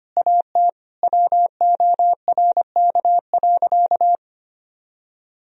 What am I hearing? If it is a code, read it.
Morse code, heard as a keyed tone, about 25 wpm, 700 Hz. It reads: AT WORK.
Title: Sentences from Top 1000 Words - Review Part 4 - 25wpm